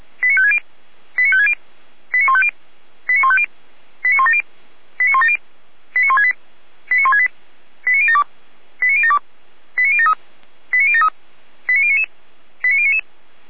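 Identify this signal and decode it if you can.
Five-tone selective call (ZVEI), heard as quick runs of five tones, tones of about 70 ms. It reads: 89560 89560 89150 89150 89170 89170 89168 89168 89072 89072 89073 89073 89099 89099
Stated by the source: Five-tone ZVEI selective call 89167 received by SDR, archived by Feuerwehr Orlamünde fire brigade.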